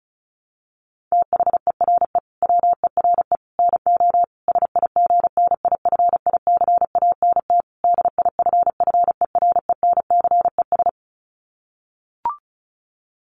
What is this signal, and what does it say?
Morse code, heard as a keyed tone, about 34 wpm, 700 Hz. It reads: THERE WERE NO SIGNIFICANT DIFFERENCES
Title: Sets of 5 Words Encore2 35wpm